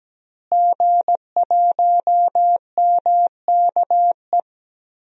Morse code, heard as a keyed tone, about 17 wpm, 700 Hz. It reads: G1MKE
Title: DXCC-100 Call Signs - 5 Characters - Repeated 3x - 17wpm